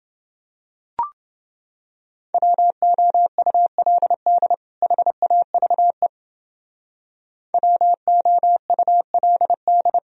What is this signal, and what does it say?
Morse code, heard as a keyed tone, about 29 wpm, 700 Hz. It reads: WOULD HAVE WOULD